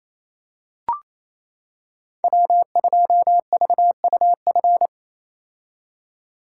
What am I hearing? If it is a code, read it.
Morse code, heard as a keyed tone, about 27 wpm, 700 Hz. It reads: W2VUF
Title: K1USN SST QSO Practice - 28wpm